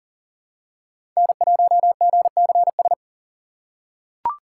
Morse code, heard as a keyed tone, about 40 wpm, 700 Hz. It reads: N1GCS
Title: USA Call Signs - 8x Character Spacing - 40wpm